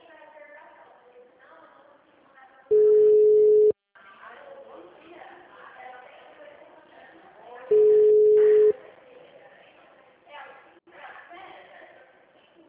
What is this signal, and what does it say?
Ringback tone (Europe, 425 Hz), with speech.